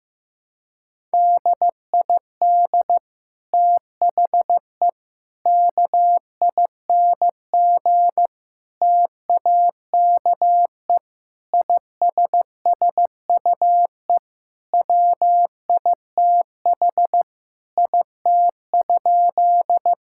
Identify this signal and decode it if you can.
Morse code, heard as a keyed tone, about 15 wpm, 700 Hz. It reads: DID THE KING TAKE ISSUE WITH IT?
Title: Sentences from Top 1000 Words - Review Part 6 - 15wpm